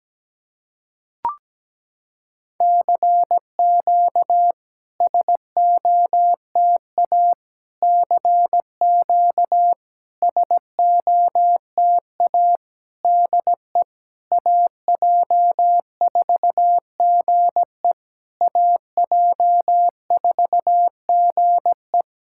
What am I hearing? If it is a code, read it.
Morse code, heard as a keyed tone, about 17 wpm, 700 Hz. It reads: CQ SOTA CQ SOTA DE AJ4GE AJ4GE
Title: SOTA QSO Practice 17wpm